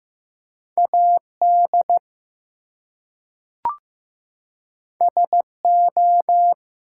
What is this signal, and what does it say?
Morse code, heard as a keyed tone, about 15 wpm, 700 Hz. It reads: AD SO
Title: Two Letter Words - 15wpm